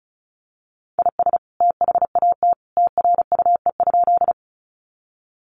Morse code, heard as a keyed tone, about 34 wpm, 700 Hz. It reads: IS THAT TRUE?
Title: Sentences from Top 400 Words 35wpm